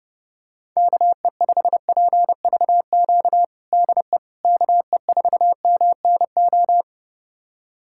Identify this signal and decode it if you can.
Morse code, heard as a keyed tone, about 30 wpm, 700 Hz. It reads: KE5PVQ DE KE4MNO
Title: USA Callsign DE Callsign 30wpm